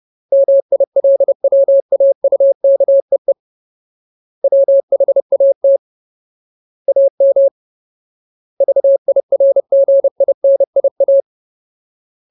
Morse code, heard as a keyed tone, about 30 wpm, 550 Hz. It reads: MILWAUKEE WHAT AM VIRGINIA
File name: Ultimate Word List / 30WPM Ultimate